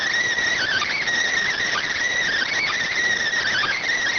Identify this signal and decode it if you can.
Emergency Alert System (SAME) tones, heard as harsh buzzing bursts. It is heard from 0.5 to 3.9 s.